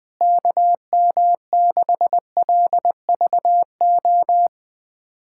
Morse code, heard as a keyed tone, about 20 wpm, 700 Hz. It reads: KM6LVO
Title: USA Callsign DE Callsign 20wpm